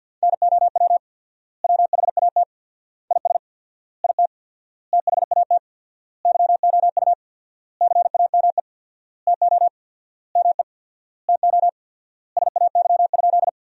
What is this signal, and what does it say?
Morse code, heard as a keyed tone, about 50 wpm, 700 Hz. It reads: NOW WHAT IS IT THAT YOU CAME TO ME TO SAY?
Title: 4x Word-Spacing - Sentences from Top 200 Words - 50wpm